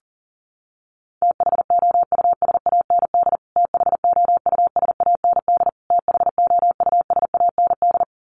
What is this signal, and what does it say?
Morse code, heard as a keyed tone, about 40 wpm, 700 Hz. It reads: THOUSAND THOUSAND THOUSAND
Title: Top 500 Words Repeated 40wpm